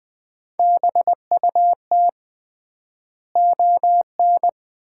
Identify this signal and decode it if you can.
Morse code, heard as a keyed tone, about 20 wpm, 700 Hz. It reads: BUT ON
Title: 3x Word-Spacing - Sets of 5 Words - 20wpm